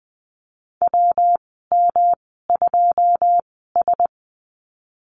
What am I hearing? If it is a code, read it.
Morse code, heard as a keyed tone, about 20 wpm, spaced as for 15 wpm, 700 Hz. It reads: WM2S
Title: 122 Lesson 20g - Number 3 - Call Signs 20wpm